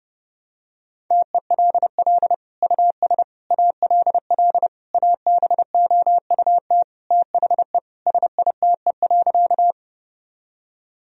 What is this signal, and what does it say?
Morse code, heard as a keyed tone, about 30 wpm, 700 Hz. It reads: TELL US ALL ABOUT THE SITE.
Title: Sentences from Top 1000 Words - Review Part 4 - 30wpm